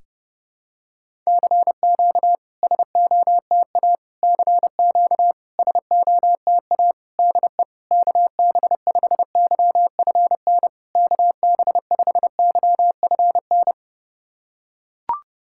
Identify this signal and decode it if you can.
Morse code, heard as a keyed tone, about 30 wpm, 690 Hz. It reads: CQ SOTA CQ SOTA DE KB5YFN KB5YFN
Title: SOTA QSO Practice 30wpm